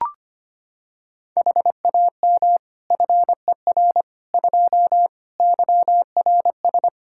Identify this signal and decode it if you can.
Morse code, heard as a keyed tone, about 25 wpm, 700 Hz. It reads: HAM FER 2 YRS
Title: Practice QSO - Exchange 3 - More Info - 25wpm